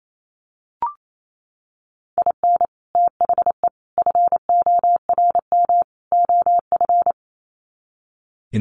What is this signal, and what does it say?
Morse code, heard as a keyed tone, about 28 wpm, 700 Hz. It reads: IN THE FORM OF
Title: Sets of 4 Words 28wpm